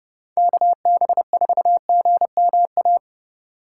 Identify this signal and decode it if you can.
Morse code, heard as a keyed tone, about 33 wpm, 700 Hz. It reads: KB4GMA